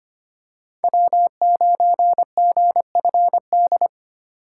Morse code, heard as a keyed tone, about 25 wpm, 700 Hz. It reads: W9GFD
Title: USA Call Signs - 6x Character Spacing - 25wpm